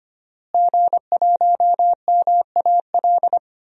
Morse code, heard as a keyed tone, about 25 wpm, 700 Hz. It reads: G1MAL
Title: DXCC-100 Call Signs - 5 Characters - Repeated 3x - 25wpm